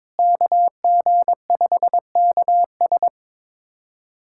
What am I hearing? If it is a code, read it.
Morse code, heard as a keyed tone, about 22 wpm, 700 Hz. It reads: KG5KS
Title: USA Callsign DE Callsign - 5 Characters - 22wpm